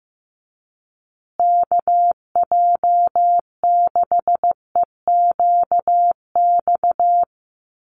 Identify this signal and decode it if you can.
Morse code, heard as a keyed tone, about 15 wpm, 700 Hz. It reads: KJ6EQX
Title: USA Call Signs Repeated 2x 15wpm